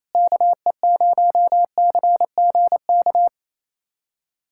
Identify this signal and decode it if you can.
Morse code, heard as a keyed tone, about 28 wpm, 700 Hz. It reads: KE0CGK
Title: USA Callsign DE Callsign 28wpm